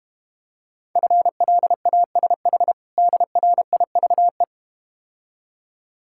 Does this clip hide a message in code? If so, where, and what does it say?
Yes: Morse code from 1.0 to 4.4 s, heard as a keyed tone, about 32 wpm, 700 Hz. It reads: FLASH DRIVE